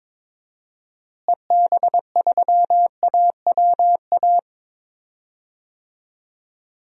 Morse code, heard as a keyed tone, about 21 wpm, 700 Hz. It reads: EB3AWA
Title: DXCC-100 Call Signs - 6 Characters - Repeated 2x - 22wpm